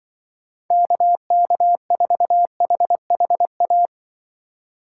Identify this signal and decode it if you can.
Morse code, heard as a keyed tone, about 23 wpm, 700 Hz. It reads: KK4HHA
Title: Speed Racing - USA 2x3 Call Signs 20wpm